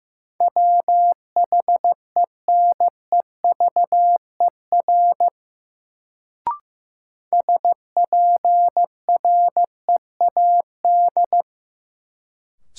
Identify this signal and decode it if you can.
Morse code, heard as a keyed tone, about 15 wpm, 700 Hz. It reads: WHENEVER SPREAD
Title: Sentences from Top 1200 Words - New Word Drill - 15wpm